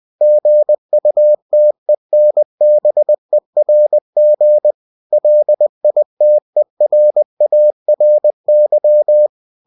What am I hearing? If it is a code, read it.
Morse code, heard as a keyed tone, about 20 wpm, 600 Hz. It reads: GUTENBERG LITERARY
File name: cw-books / winnie-the-pooh / winnie-20wpm-page-0114